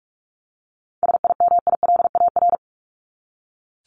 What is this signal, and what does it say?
Morse code, heard as a keyed tone, about 44 wpm, 700 Hz. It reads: SIMILAR
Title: LY Suffix - Top 100 Words with Base Words 45wpm